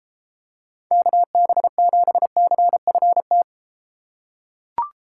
Morse code, heard as a keyed tone, about 33 wpm, 700 Hz. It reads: KB7CFT